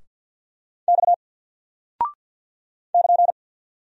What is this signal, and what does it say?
Morse code, heard as a keyed tone, about 48 wpm, 700 Hz. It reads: X <KN>
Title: Single Character (Letter:Number:Punctuation:Prosign) 50wpm